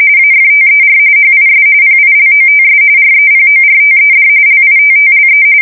RTTY, about 45.45 baud. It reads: AFTER HIS PILOTS REPORTED LITTLE
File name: fsk170 / battleshipnj